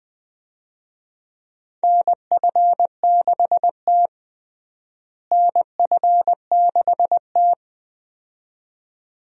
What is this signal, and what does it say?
Morse code, heard as a keyed tone, about 20 wpm, 700 Hz. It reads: NF6T NF6T
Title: USA Call Signs - 4 Characters - Repeated 2x - 20wpm